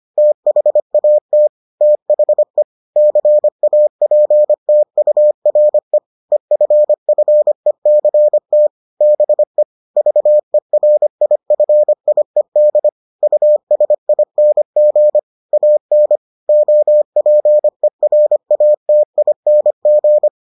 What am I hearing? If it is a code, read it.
Morse code, heard as a keyed tone, about 25 wpm, 600 Hz. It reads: THAT THE CAPTURE EFFECT BE VERIFIED USING AN OPERATING